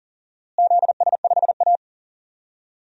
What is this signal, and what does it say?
Morse code, heard as a keyed tone, about 40 wpm, 700 Hz. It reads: ZS5A
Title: DXCC-100 Call Signs - 4 Characters - Repeated 2x - 40wpm